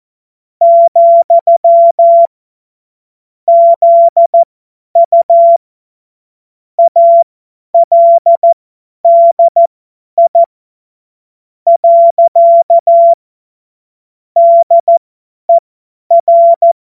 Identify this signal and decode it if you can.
Morse code, heard as a keyed tone, about 14 wpm, spaced as for 10 wpm, 680 Hz. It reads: , ZU ALDI . DER
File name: Witz-30-September-2025-50-F70